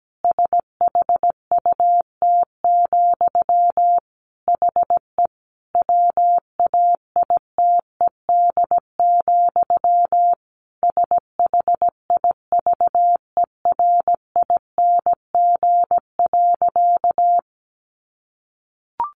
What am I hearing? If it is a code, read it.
Morse code, heard as a keyed tone, about 17 wpm, 700 Hz. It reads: SHUT, HE WAITED, SHIVERING.
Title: A Sound Of Thunder 17wpm